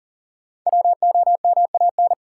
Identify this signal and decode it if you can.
Morse code, heard as a keyed tone, about 40 wpm, 700 Hz. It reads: WOMAN
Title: Sentences from Top 1000 Words - Review Part 2 - New Word Drill - 40wpm